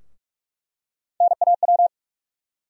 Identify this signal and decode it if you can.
Morse code, heard as a keyed tone, about 46 wpm, 700 Hz. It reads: NAW
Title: USA Call Sign Postfixes - 3 Letters 45wpm